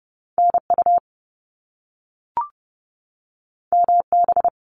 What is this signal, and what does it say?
Morse code, heard as a keyed tone, about 30 wpm, 700 Hz. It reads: NU MB
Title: Canadian Provinces and Territories 30wpm